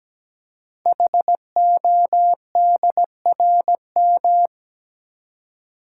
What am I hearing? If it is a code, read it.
Morse code, heard as a keyed tone, about 17 wpm, 700 Hz. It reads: HODRM
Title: Code Groups - 5 Letters - 17wpm